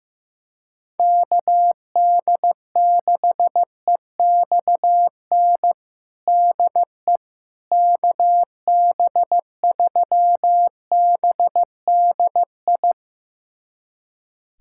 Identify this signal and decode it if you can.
Morse code, heard as a keyed tone, about 16 wpm, 700 Hz. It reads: KD6EXN DE KB3BDI
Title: USA Callsign DE Callsign 15wpm